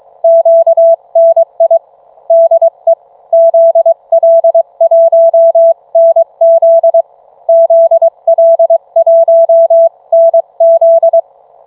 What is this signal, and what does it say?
Morse code, heard as a keyed tone, about 23 wpm, 660 Hz. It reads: QNI DE ZL1NZ ZL1NZ